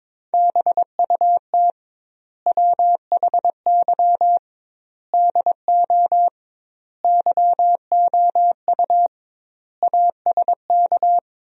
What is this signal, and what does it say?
Morse code, heard as a keyed tone, about 22 wpm, 700 Hz. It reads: BUT WHY DO YOU ASK
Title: 2x Word-Spacing - Sentences from Top 200 Words - 22wpm